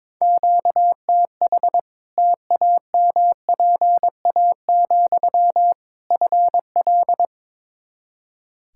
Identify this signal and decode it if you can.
Morse code, heard as a keyed tone, about 22 wpm, 700 Hz. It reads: QTH TAMPA, FL
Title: Practice QSO - Exchange 1 - Beginning - 22wpm